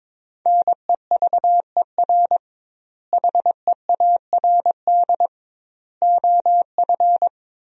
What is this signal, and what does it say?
Morse code, heard as a keyed tone, about 22 wpm, 700 Hz. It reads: NEVER HEARD OF
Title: 2x Word-Spacing - Sentences from Top 400 Words - 22wpm